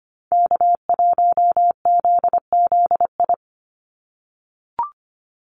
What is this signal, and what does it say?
Morse code, heard as a keyed tone, about 25 wpm, 700 Hz. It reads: K1ZZI
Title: Call Signs - 25wpm